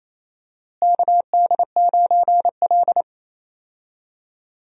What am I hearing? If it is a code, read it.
Morse code, heard as a keyed tone, about 28 wpm, 700 Hz. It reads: KD9L
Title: USA Call Signs - 4 Characters - Repeated 3x - 28wpm